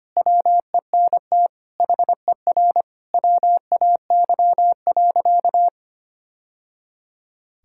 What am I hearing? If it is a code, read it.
Morse code, heard as a keyed tone, about 25 wpm, 700 Hz. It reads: WENT HER WAY.
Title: Sentences from Top 1000 Words - Review Part 5 - 25wpm